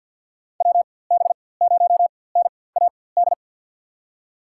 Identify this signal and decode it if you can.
Morse code, heard as a keyed tone, about 46 wpm, spaced as for 23 wpm, 700 Hz. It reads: WB0NAD
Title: USA Call Signs - 4x Character Spacing - 50wpm